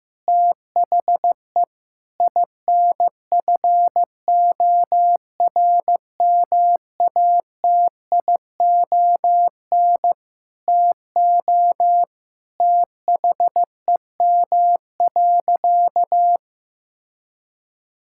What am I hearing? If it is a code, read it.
Morse code, heard as a keyed tone, about 15 wpm, 700 Hz. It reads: THE INFORMATION TO THEM.
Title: Sentences from Top 1000 Words - Review Part 5 - 15wpm